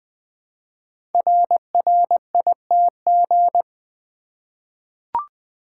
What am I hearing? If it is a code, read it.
Morse code, heard as a keyed tone, about 20 wpm, 700 Hz. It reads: RRITG
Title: Code Groups - 5 Letter:Numbers - 20wpm